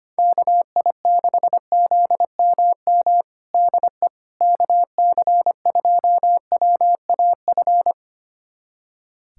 Morse code, heard as a keyed tone, about 25 wpm, 700 Hz. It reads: KI6ZMM DE KC2WAF
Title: USA Callsign DE Callsign 25wpm